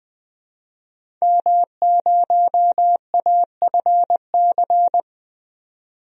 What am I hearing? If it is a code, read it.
Morse code, heard as a keyed tone, about 20 wpm, 700 Hz. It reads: M0AFC
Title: DXCC-100 Call Signs - 5 Characters - 20wpm